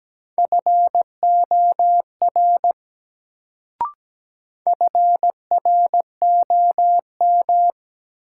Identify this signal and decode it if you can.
Morse code, heard as a keyed tone, about 17 wpm, 700 Hz. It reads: FOR FROM
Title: Sentences from Top 100 Words - New Word Drill - 17wpm